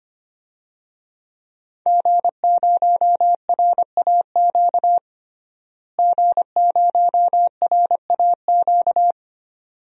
Morse code, heard as a keyed tone, about 25 wpm, 700 Hz. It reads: G0RAQ G0RAQ
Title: DXCC-100 Call Signs - 5 Characters - Repeated 2x - 25wpm